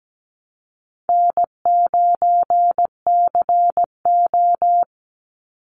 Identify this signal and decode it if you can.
Morse code, heard as a keyed tone, about 17 wpm, 700 Hz. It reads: N9CO
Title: Call Signs - 17wpm